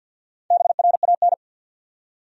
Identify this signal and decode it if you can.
Morse code, heard as a keyed tone, about 51 wpm, 700 Hz. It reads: BRAN
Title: USA Common Names - Max 4 Letters - 50wpm